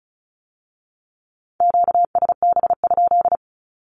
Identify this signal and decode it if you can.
Morse code, heard as a keyed tone, about 36 wpm, 700 Hz. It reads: QSB?
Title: Q-Codes with Spoken Meaning - 35wpm